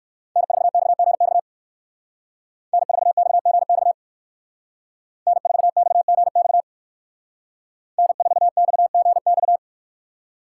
Morse code, heard as a keyed tone, about 62 wpm, 700 Hz. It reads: N4XGX N4XGX N4XGX N4XGX